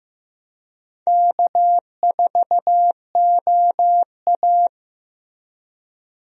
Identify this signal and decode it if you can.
Morse code, heard as a keyed tone, about 15 wpm, 700 Hz. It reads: K4OA
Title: USA Call Signs - 4 Characters - Repeated 2x - 15wpm